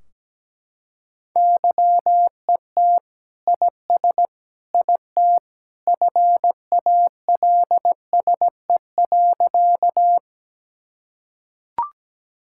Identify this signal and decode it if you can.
Morse code, heard as a keyed tone, about 17 wpm, 700 Hz. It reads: YET IS IT FALSE.
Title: Sentences from Top 1600 Words 17wpm